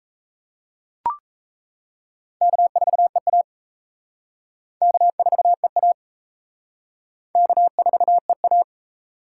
Morse code, heard as a keyed tone, about 35 wpm, 700 Hz. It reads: K4EA K4EA K4EA
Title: Speed Racing - USA 1x2 Call Signs 28wpm